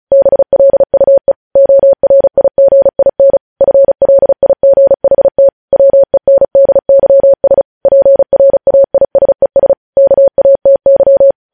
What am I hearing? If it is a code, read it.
Morse code, heard as a keyed tone, about 36 wpm, 550 Hz. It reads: BLUE ORIGIN FLIGHT WENDYS PRAISES KATY
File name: podcast / News headlines-1x-35x35